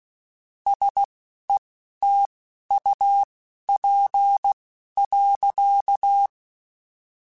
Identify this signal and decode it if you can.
Morse code, heard as a keyed tone, about 16 wpm, spaced as for 12 wpm, 800 Hz. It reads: SETUP.
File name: QSOs / QSOs-mp3s / WD0BGP 1@15